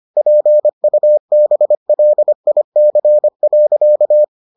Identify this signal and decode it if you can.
Morse code, heard as a keyed tone, about 25 wpm, 600 Hz. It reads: PUBLIC.